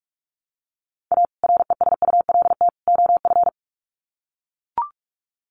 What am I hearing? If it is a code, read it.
Morse code, heard as a keyed tone, about 44 wpm, 700 Hz. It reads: A RESULT OF
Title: Sets of 3 Words 45wpm